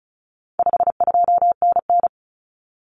Morse code, heard as a keyed tone, about 34 wpm, 700 Hz. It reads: 52NN
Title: Common Cut Numbers - 4-Digit 35wpm